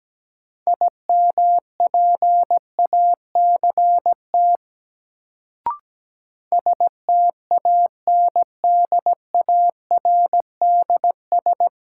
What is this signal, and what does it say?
Morse code, heard as a keyed tone, about 17 wpm, 700 Hz. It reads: IMPACT STANDARDS
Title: Sentences from Top 1000 Words - Review Part 6 - New Word Drill - 17wpm